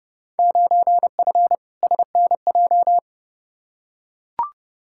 Morse code, heard as a keyed tone, about 32 wpm, 700 Hz. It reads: 9F SNJ
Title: Field Day Exchanges Encore - 30wpm